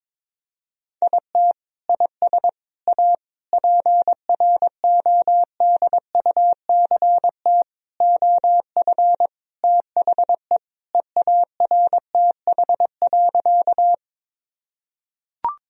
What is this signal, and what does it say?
Morse code, heard as a keyed tone, about 22 wpm, 700 Hz. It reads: IT IS A PRODUCT OF THE EARTH.
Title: Sentences from Top 1000 Words - Review Part 2 - 22wpm